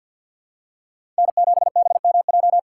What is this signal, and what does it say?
Morse code, heard as a keyed tone, about 51 wpm, 700 Hz. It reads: N7BMJ
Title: USA Call Signs - 5 Characters - 6x Character Spacing - 50wpm